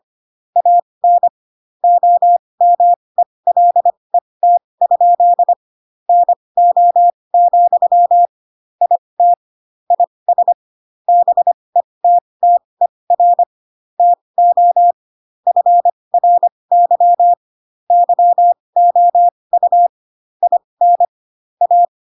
Morse code, heard as a keyed tone, about 25 wpm, spaced as for 20 wpm, 700 Hz. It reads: AN OMELET? NO, IT IS BETTER TO FRY YOU IN A